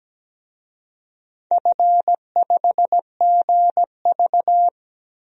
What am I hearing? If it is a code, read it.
Morse code, heard as a keyed tone, about 17 wpm, 700 Hz. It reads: F5GV
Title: DXCC-100 Call Signs - 4 Characters - 17wpm